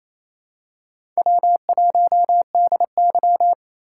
Morse code, heard as a keyed tone, about 28 wpm, 700 Hz. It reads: W1DY
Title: USA Call Signs - 2x Character Spacing - 28wpm